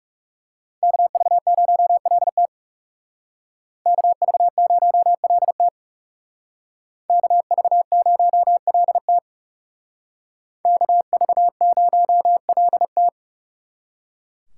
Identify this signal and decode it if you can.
Morse code, heard as a keyed tone, about 35 wpm, 700 Hz. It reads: KV0LT KV0LT KV0LT KV0LT
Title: Speed Racing - USA Call Signs 30wpm